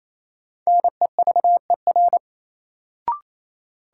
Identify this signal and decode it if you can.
Morse code, heard as a keyed tone, about 28 wpm, 700 Hz. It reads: NEVER
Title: Sentences from Top 300 Words - New Word Drill - 28wpm